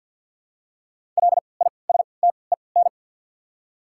Morse code, heard as a keyed tone, about 53 wpm, spaced as for 29 wpm, 690 Hz. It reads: LISTEN
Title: 3x Character Spacing - Top 500 Words - 50wpm